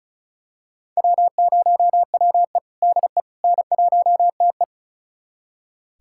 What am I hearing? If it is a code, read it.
Morse code, heard as a keyed tone, about 36 wpm, 700 Hz. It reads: W0WE DE N1TE
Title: USA Callsign DE Callsign - 4 Characters - 35wpm